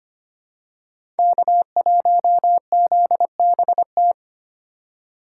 Morse code, heard as a keyed tone, about 25 wpm, 700 Hz. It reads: K1ZBT
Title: USA Call Signs - 2x Character Spacing - 25wpm